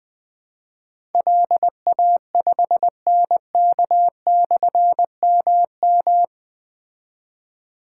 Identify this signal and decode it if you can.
Morse code, heard as a keyed tone, about 20 wpm, 700 Hz. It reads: LA5NK/MM
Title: Difficult Call Signs 20wpm